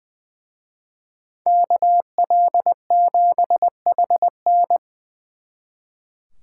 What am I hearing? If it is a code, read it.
Morse code, heard as a keyed tone, about 23 wpm, 700 Hz. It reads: KL7HN
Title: Speed Racing - USA 2x2 Call Signs 20wpm